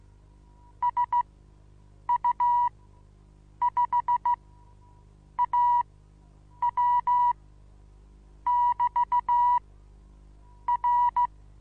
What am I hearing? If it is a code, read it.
Morse code, heard as a keyed tone, about 16 wpm, spaced as for 7 wpm, 970 Hz. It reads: SU5AW <BT> R
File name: Onlinelehrgang / morsen / DJ4UF Morsen25